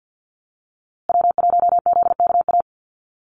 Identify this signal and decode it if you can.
Morse code, heard as a keyed tone, about 50 wpm, 700 Hz. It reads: W1ZKA
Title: USA Call Signs Repeated 3x 50wpm